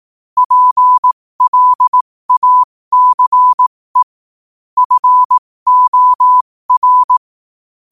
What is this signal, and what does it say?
Morse code, heard as a keyed tone, about 18 wpm, spaced as for 15 wpm, 1000 Hz. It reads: PLACE FOR
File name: qotdfiles / 15wpm